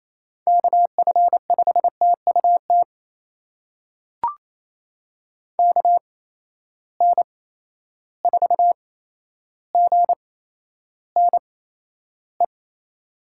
Morse code, heard as a keyed tone, about 28 wpm, spaced as for 12 wpm, 700 Hz. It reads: KF5TUT KN4GNE
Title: USA Call Signs - 8x Character Spacing - 28wpm